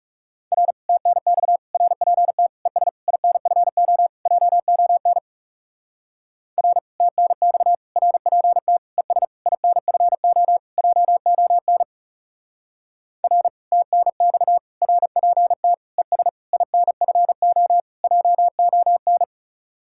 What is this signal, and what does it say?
Morse code, heard as a keyed tone, about 37 wpm, 700 Hz. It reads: R TNX RPT ES INFO JON R TNX RPT ES INFO JON R TNX RPT ES INFO JON
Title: Speed-Racing Common QSO Phrases - 30wpm